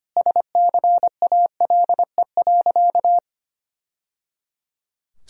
Morse code, heard as a keyed tone, about 25 wpm, 700 Hz. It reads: SCALE.
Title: Sentences from Top 1000 Words - Review Part 5 - 25wpm